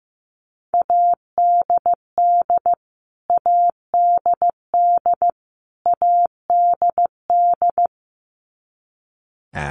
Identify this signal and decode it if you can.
Morse code, heard as a keyed tone, about 15 wpm, 700 Hz. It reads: ADD ADD ADD
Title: Top 200 Words Repeated 15wpm